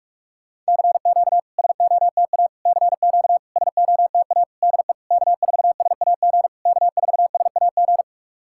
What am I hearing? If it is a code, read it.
Morse code, heard as a keyed tone, about 44 wpm, 700 Hz. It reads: CQ SOTA CQ SOTA DE K4SAG K4SAG
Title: SOTA QSO Practice 45wpm